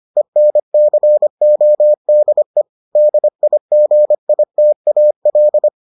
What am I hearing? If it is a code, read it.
Morse code, heard as a keyed tone, about 25 wpm, 600 Hz. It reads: ENCODE DIGITAL